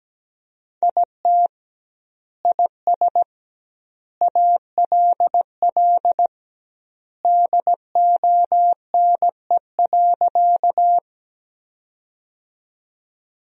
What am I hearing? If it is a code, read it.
Morse code, heard as a keyed tone, about 17 wpm, 700 Hz. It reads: IT IS ALL DONE.
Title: 2x Word-Spacing - Sentences from Top 500 Words - 17wpm